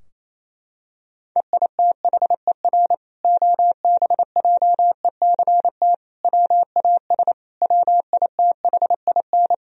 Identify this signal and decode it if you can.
Morse code, heard as a keyed tone, about 28 wpm, 700 Hz. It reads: EITHER OBJECT WAS WITHIN
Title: Sentences from Top 1000 Words - Review Part 4 - 28wpm